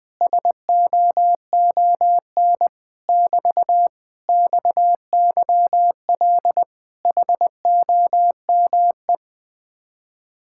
Morse code, heard as a keyed tone, about 20 wpm, 700 Hz. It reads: SOON <BT> XYL HOME
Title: Speed-Racing Common QSO Phrases Encore - 17wpm